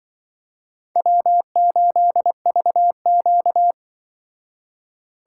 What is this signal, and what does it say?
Morse code, heard as a keyed tone, about 24 wpm, 700 Hz. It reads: W8VQ
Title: Speed Racing - USA 1x2 Call Signs 20wpm